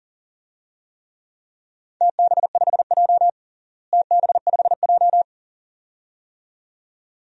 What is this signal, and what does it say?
Morse code, heard as a keyed tone, about 40 wpm, 700 Hz. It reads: TB5J TB5J
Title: DXCC-100 Call Signs - 4 Characters - Repeated 2x - 40wpm